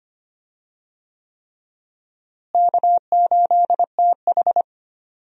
Morse code, heard as a keyed tone, about 25 wpm, 700 Hz. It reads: K8TH
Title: USA Call Signs - 4 Characters - Repeated 3x - 25wpm